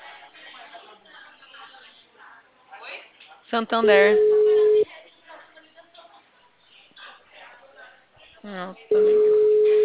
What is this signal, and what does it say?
Ringback tone (Europe, 425 Hz), with speech.
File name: out- -2030- - - .78